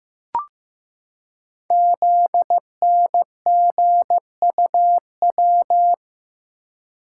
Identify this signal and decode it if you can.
Morse code, heard as a keyed tone, about 15 wpm, 700 Hz. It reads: ZNGUW
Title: Code Groups - 5 Letters - 15wpm